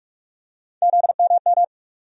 Morse code, heard as a keyed tone, about 42 wpm, 700 Hz. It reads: ZMM
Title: USA Call Sign Postfixes - 3 Letters 45wpm